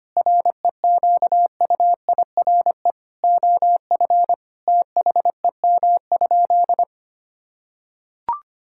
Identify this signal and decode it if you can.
Morse code, heard as a keyed tone, about 25 wpm, 700 Hz. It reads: REQUIRE OF THEM?
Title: Sentences from Top 1000 Words - Review Part 6 - 25wpm